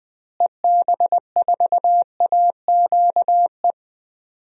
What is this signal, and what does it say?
Morse code, heard as a keyed tone, about 20 wpm, 700 Hz. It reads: EB4AQE
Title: DXCC-100 Call Signs - 6 Characters - Repeated 2x - 20wpm